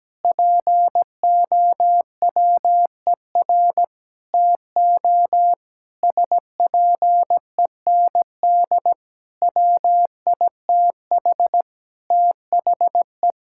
Morse code, heard as a keyed tone, about 17 wpm, 700 Hz. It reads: POWER TO SPEND WITH THE
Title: Sentences from Top 1000 Words - Review Part 5 - 17wpm